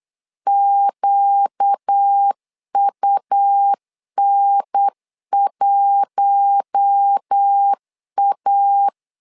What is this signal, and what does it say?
Morse code, heard as a keyed tone, about 8 wpm, 800 Hz. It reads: QUN1A
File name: LEZIONI / 5 LBVC / 5 L 8wpm 2831